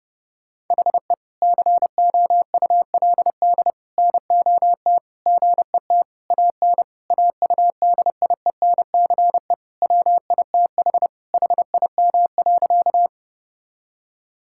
Morse code, heard as a keyed tone, about 32 wpm, 700 Hz. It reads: HE COULD NOT GET AN AUDIENCE WITH HIM.
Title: Sentences from Top 1000 Words - Review Part 5 - 30wpm